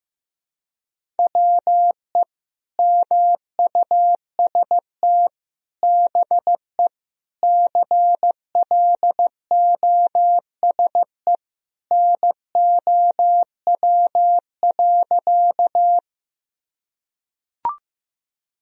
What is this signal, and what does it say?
Morse code, heard as a keyed tone, about 16 wpm, 700 Hz. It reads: WE MUST BE CLOSE NOW.
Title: Sentences from Top 1000 Words - Review Part 1 - 15wpm